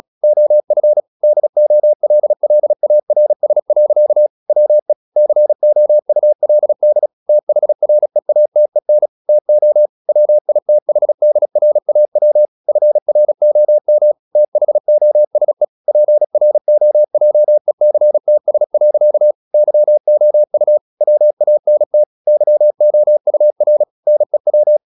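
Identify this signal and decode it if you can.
Morse code, heard as a keyed tone, about 36 wpm, 600 Hz. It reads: OF DOLLARS. WE COULD THREATEN TO WITHDRAW FROM THOSE PROJECTS. YOU WANT YOUR NEW
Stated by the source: Morse practice at 36 wpm, text of Ken Follett's novel Never, from CW med Gnister.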